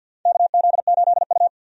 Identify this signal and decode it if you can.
Morse code, heard as a keyed tone, about 52 wpm, 700 Hz. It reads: KZ8U
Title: USA Call Signs - 4 Characters - 50wpm